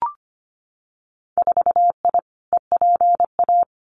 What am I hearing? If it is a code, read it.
Morse code, heard as a keyed tone, about 25 wpm, 700 Hz. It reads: 4I EPA
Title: Winter Field Day Exchanges Encore 25wpm